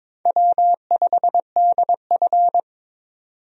Morse code, heard as a keyed tone, about 22 wpm, 700 Hz. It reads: W5DF
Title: USA Call Signs - 4 Characters - 22wpm